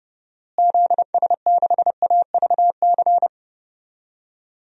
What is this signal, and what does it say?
Morse code, heard as a keyed tone, about 30 wpm, 700 Hz. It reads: ZS6AVC
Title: DXCC-100 Call Signs - 6 Characters - Repeated 3x - 30wpm